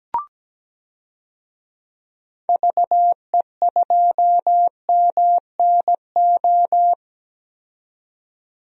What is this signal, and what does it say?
Morse code, heard as a keyed tone, about 17 wpm, 700 Hz. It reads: VE2MNO
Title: DXCC-100 Call Signs - 6 Characters - 17wpm